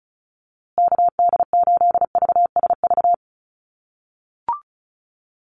Morse code, heard as a keyed tone, about 34 wpm, 700 Hz. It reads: KD8VSV
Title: USA Call Signs Repeated 3x 35wpm